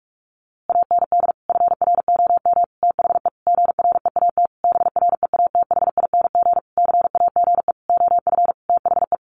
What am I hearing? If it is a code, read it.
Morse code, heard as a keyed tone, about 46 wpm, 700 Hz. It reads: AND FROM THE GREAT BREATHING CAGE OF THE
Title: A Sound Of Thunder 45wpm